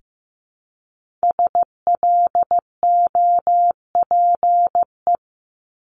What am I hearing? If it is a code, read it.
Morse code, heard as a keyed tone, about 15 wpm, 700 Hz. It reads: SLOPE
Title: Multi-letter Phonemes Repeated 3x 15wpm